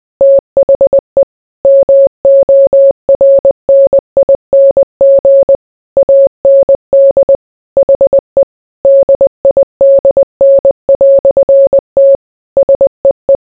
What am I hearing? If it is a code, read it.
Morse code, heard as a keyed tone, about 20 wpm, 550 Hz. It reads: THE MORNING AND HE DIDN"T SEE